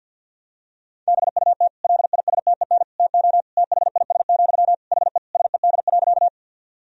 Morse code, heard as a keyed tone, about 51 wpm, 700 Hz. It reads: BUT LISTEN TO THIS, HE SED.
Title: Sentences from Top 400 Words and Abbreviations 50wpm